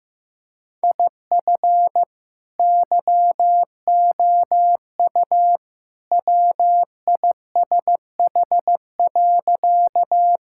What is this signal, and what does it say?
Morse code, heard as a keyed tone, about 15 wpm, 700 Hz. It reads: IF YOU WISH.
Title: Sentences from Top 1600 Words 15wpm